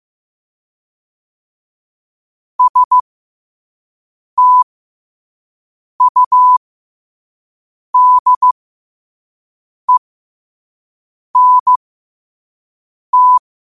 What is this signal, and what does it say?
Morse code, heard as a keyed tone, about 15 wpm, spaced as for 5 wpm, 1000 Hz. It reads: STUDENT
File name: qotdfiles / 5wpm